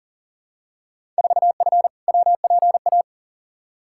Morse code, heard as a keyed tone, about 40 wpm, 700 Hz. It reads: 4F WPA
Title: Field Day Exchanges - 40wpm